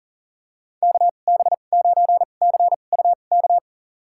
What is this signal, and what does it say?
Morse code, heard as a keyed tone, about 38 wpm, spaced as for 28 wpm, 700 Hz. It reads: KB9CUK
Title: USA Call Signs - 6 Characters - 2x Character Spacing - 40wpm